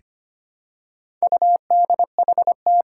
Morse code, heard as a keyed tone, about 25 wpm, 700 Hz. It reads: UDHT
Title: Four Challenging Letter-Number Combinations 25wpm